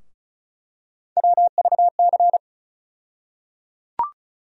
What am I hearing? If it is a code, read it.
Morse code, heard as a keyed tone, about 36 wpm, 700 Hz. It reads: WVC
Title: USA Call Sign Postfixes - 3 Letters 35wpm